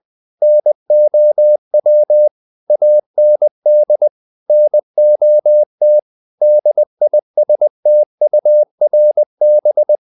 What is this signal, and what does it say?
Morse code, heard as a keyed tone, about 20 wpm, 600 Hz. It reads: NOW AND NOT DISTURB
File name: the old man and the sea part 2 20wpm